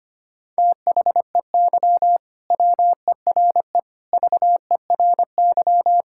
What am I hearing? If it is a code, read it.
Morse code, heard as a keyed tone, about 25 wpm, 700 Hz. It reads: THEY WERE VERY
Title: Sentences from Top 1000 Words - Review Part 5 - 25wpm